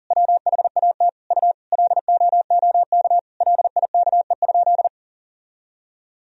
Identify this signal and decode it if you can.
Morse code, heard as a keyed tone, about 37 wpm, 700 Hz. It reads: WHAT U LOOK LIKE?
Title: Sentences from Top 100 Words and Abbreviations 40wpm